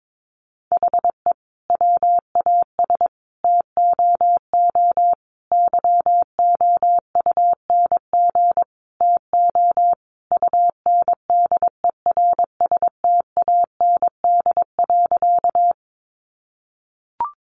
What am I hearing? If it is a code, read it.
Morse code, heard as a keyed tone, about 22 wpm, 700 Hz. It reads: HE WAS TOO YOUNG TO UNDERSTAND.
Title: Sentences from Top 600 Words 22wpm